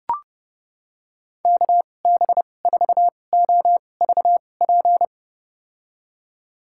Morse code, heard as a keyed tone, about 32 wpm, spaced as for 22 wpm, 700 Hz. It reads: KB4OVP
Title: USA Call Signs - 2x Character Spacing - 30wpm